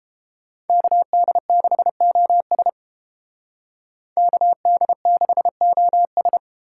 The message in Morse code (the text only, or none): KD6OS KD6OS